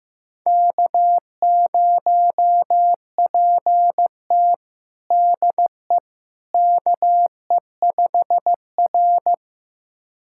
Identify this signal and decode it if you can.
Morse code, heard as a keyed tone, about 15 wpm, 700 Hz. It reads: K0PT DE KE5R
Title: USA Callsign DE Callsign - 4 Characters - 15wpm